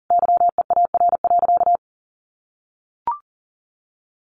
Morse code, heard as a keyed tone, about 40 wpm, 700 Hz. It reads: YEAR.
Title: Sentences from Top 1500 Words 40wpm